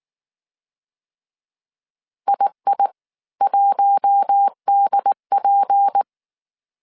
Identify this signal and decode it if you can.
Morse code, heard as a keyed tone, about 19 wpm, spaced as for 16 wpm, 800 Hz. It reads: II1DP
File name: CW / DET / 1 6 1500C S C 18wpm 2356